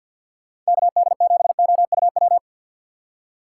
Morse code, heard as a keyed tone, about 61 wpm, 700 Hz. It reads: KD7OUW
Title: Speed Racing - USA 2x3 Call Signs 50wpm